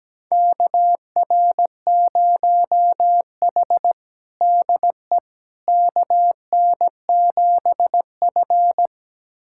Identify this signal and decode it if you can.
Morse code, heard as a keyed tone, about 17 wpm, 700 Hz. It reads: KR0H DE KN7F
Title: USA Callsign DE Callsign - 4 Characters - 17wpm